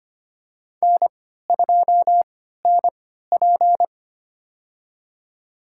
Morse code, heard as a keyed tone, about 25 wpm, spaced as for 14 wpm, 700 Hz. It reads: N2NP
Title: USA Call Signs - 4 Characters - 3x Character Spacing - 25wpm